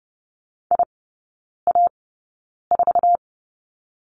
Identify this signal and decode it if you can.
Morse code, heard as a keyed tone, about 30 wpm, spaced as for 9 wpm, 700 Hz. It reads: IA4
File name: course / 30:10wpm / 087 Lesson 15e - Number 5 - Three Characters 30wpm